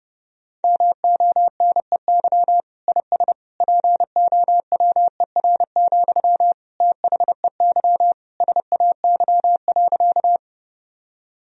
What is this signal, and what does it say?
Morse code, heard as a keyed tone, about 30 wpm, 700 Hz. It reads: MONEY IS POWER, THEY SAY.
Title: Sentences from Top 1000 Words - Review Part 2 - 30wpm